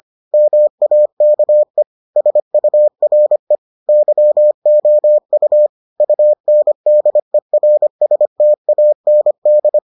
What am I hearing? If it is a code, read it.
Morse code, heard as a keyed tone, about 25 wpm, 600 Hz. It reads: MAKE SURE YOU UNDERSTAND